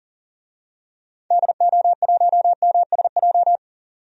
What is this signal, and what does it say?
Morse code, heard as a keyed tone, about 40 wpm, 700 Hz. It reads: DO1MSJ
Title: DXCC-100 Call Signs - 6 Characters - 40wpm